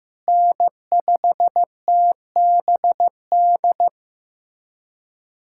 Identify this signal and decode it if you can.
Morse code, heard as a keyed tone, about 15 wpm, 700 Hz. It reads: N5TBD
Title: USA Callsign DE Callsign - 5 Characters - 15wpm